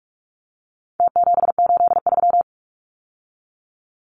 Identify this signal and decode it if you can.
Morse code, heard as a keyed tone, about 44 wpm, 700 Hz. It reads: T783
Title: Numbers - Contest Style - 4-Digit 45wpm